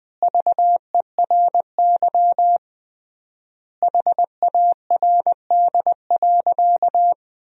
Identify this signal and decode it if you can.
Morse code, heard as a keyed tone, about 21 wpm, 700 Hz. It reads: VERY HARD.
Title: 3x Word-Spacing - Sentences from Top 300 Words - 20wpm